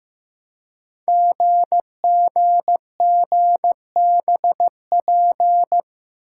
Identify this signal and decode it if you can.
Morse code, heard as a keyed tone, about 15 wpm, 700 Hz. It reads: GGGBP
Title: Code Groups - 5 Letters - 15wpm